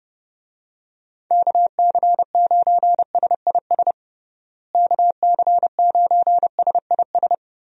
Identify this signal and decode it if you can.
Morse code, heard as a keyed tone, about 30 wpm, 700 Hz. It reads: KC9SIS KC9SIS
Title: USA Call Signs - 6 Characters - Repeated 2x - 30wpm